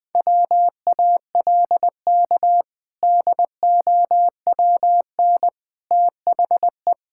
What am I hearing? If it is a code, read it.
Morse code, heard as a keyed tone, about 20 wpm, 700 Hz. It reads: WALK DOWN THE
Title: Sentences with 2-Word Phrases - 20wpm